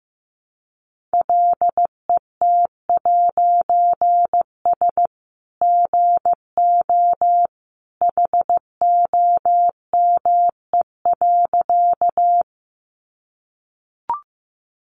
Morse code, heard as a keyed tone, about 15 wpm, 700 Hz. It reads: LET'S GO HOME.
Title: A Sound Of Thunder 15wpm